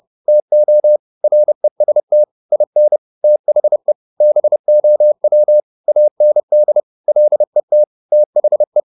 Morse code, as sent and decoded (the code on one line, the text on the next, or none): - --- / .-. ... - / .. -. / - .... / -... --- .-- / .- -. -.. / .-.. - / - .... .
TO REST IN THE BOW AND LET THE